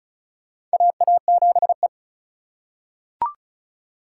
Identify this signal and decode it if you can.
Morse code, heard as a keyed tone, about 36 wpm, 700 Hz. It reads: AA7E
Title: USA Call Signs - 4 Characters - 2x Character Spacing - 35wpm